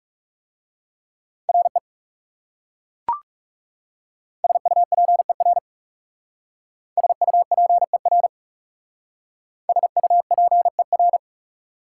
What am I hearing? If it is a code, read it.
Morse code, heard as a keyed tone, about 40 wpm, 690 Hz. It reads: RE SUPER SUPER SUPER